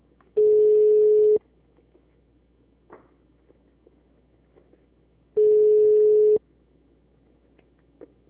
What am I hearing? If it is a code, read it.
Ringback tone (Europe, 425 Hz).
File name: out- -2005- - - .441